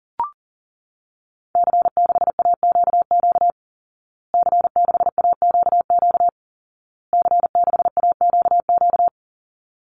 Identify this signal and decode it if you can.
Morse code, heard as a keyed tone, about 40 wpm, 700 Hz. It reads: C6AQQ C6AQQ C6AQQ
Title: Call Signs Repeated 2x - 40wpm